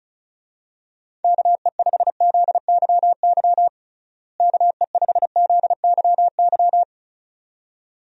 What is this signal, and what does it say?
Morse code, heard as a keyed tone, about 36 wpm, 700 Hz. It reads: KE5ZYY KE5ZYY
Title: USA Call Signs - 6 Characters - Repeated 2x - 35wpm